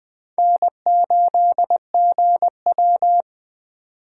Morse code, heard as a keyed tone, about 20 wpm, 700 Hz. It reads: N8GW